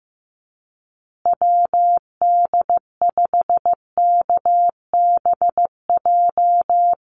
Morse code, heard as a keyed tone, about 15 wpm, 700 Hz. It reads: WD5KBJ
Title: USA Call Signs Repeated 2x 15wpm